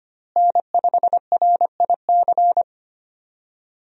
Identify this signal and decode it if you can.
Morse code, heard as a keyed tone, about 25 wpm, 700 Hz. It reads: N5RIC
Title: USA Call Signs - 5 Characters - Repeated 3x - 25wpm